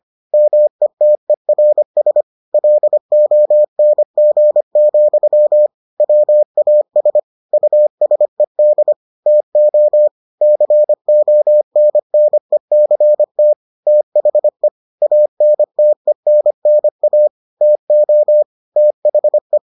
Morse code, heard as a keyed tone, about 25 wpm, 600 Hz. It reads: METERS LONG, WAS USED TO CONNECT THE ANTENNA TO THE